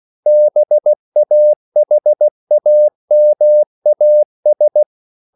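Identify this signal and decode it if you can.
Morse code, heard as a keyed tone, about 16 wpm, 600 Hz. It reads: BAHAMAS
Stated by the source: CW practice of national capitals at 16 words per minute, from ON6ZQ.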